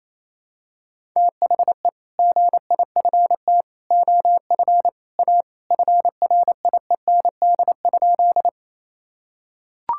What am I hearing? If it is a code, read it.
Morse code, heard as a keyed tone, about 28 wpm, 700 Hz. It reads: THE GIFT OF A FRIEND?
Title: Sentences from Top 1000 Words - Review Part 6 - 28wpm